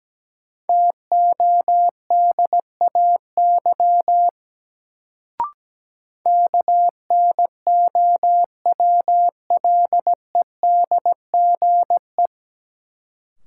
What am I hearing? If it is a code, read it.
Morse code, heard as a keyed tone, about 17 wpm, 700 Hz. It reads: TODAY KNOWLEDGE
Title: Sentences from Top 600 Words - New Word Drill - 17wpm